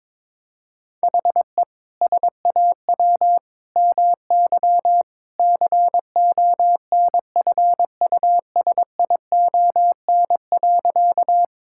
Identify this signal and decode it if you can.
Morse code, heard as a keyed tone, about 22 wpm, 700 Hz. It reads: HE SAW MY CONFUSION.
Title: Sentences from Top 1600 Words 22wpm